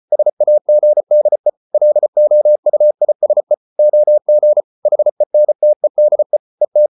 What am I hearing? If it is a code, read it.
Morse code, heard as a keyed tone, about 34 wpm, 600 Hz. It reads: SAGDE LOUISE OG HENTEDE ET
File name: Ferie QRQ 34wpm